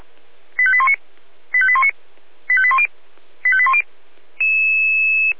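Five-tone selective call (ZVEI), heard as quick runs of five tones, tones of about 70 ms. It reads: 85819 85819 85810 85810
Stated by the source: Five-tone ZVEI selective call received on SDR.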